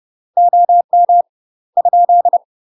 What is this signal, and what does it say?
Morse code, heard as a keyed tone, about 30 wpm, 700 Hz. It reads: OM ?
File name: UK MASTER QSOs TRAINING / 30WPM / UK MASTER QSOs 0009